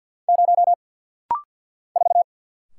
Morse code, heard as a keyed tone, about 50 wpm, 700 Hz. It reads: 0 4
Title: Single Letter-Number - Rapid-Fire 50wpm